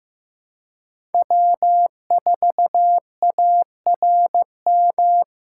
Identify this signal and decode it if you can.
Morse code, heard as a keyed tone, about 15 wpm, 700 Hz. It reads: W4ARM
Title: USA Call Signs - 5 Characters - 3x Character Spacing - 15wpm